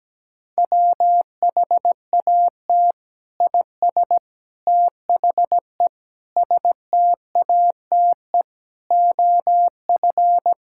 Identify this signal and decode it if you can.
Morse code, heard as a keyed tone, about 17 wpm, 700 Hz. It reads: WHAT IS THE STATE OF
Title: Sentences from Top 1000 Words - Review Part 5 - 17wpm